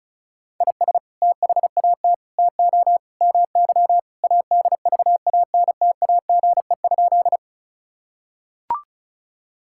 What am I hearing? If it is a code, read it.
Morse code, heard as a keyed tone, about 36 wpm, 700 Hz. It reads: IS THAT TO MY ADVANTAGE?
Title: Sentences from Top 1000 Words - Review Part 6 - 35wpm